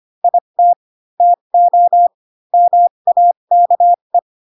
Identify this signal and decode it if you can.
Morse code, heard as a keyed tone, about 25 wpm, spaced as for 22 wpm, 700 Hz. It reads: IT TO MAKE
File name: Pinocchio / Pinocchio ch2 18 25 ik1hge